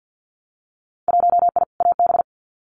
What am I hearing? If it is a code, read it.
Morse code, heard as a keyed tone, about 50 wpm, 700 Hz. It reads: 1I AB
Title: Winter Field Day Exchanges Encore 50wpm